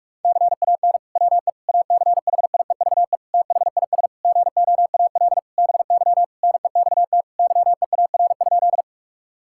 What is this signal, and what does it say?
Morse code, heard as a keyed tone, about 46 wpm, 700 Hz. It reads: CAN WE ACHIEVE THIS GOAL BY NEXT YEAR?
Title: Sentences from Top 1000 Words - Review Part 5 - 45wpm